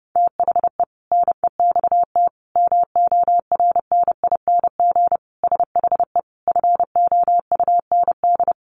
Morse code, heard as a keyed tone, about 30 wpm, 700 Hz. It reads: THE NEXT MORNING SHE FOUND
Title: Sentences from Top 1200 Words 30wpm